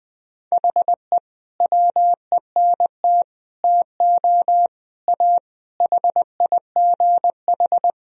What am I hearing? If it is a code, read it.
Morse code, heard as a keyed tone, about 20 wpm, 700 Hz. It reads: HE WENT TO A HIGH
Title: Sentences from Top 1000 Words - Review Part 1 - 20wpm